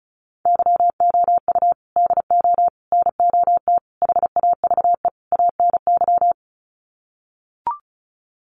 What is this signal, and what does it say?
Morse code, heard as a keyed tone, about 34 wpm, 700 Hz. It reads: YOU DO NOT HAVE ANY
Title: Sets of 5 Words Encore 35wpm